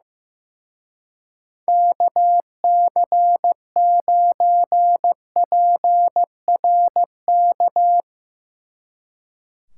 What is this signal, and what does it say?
Morse code, heard as a keyed tone, about 16 wpm, 700 Hz. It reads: KC9PRK
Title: Speed Racing - USA 2x3 Call Signs 15wpm